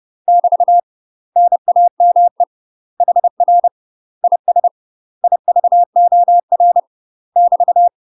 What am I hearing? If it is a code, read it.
Morse code, heard as a keyed tone, about 30 wpm, 700 Hz. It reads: <BT> NAME HR IS IVOR <BT>
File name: UK MASTER QSOs TRAINING / 30WPM / UK MASTER QSOs 0002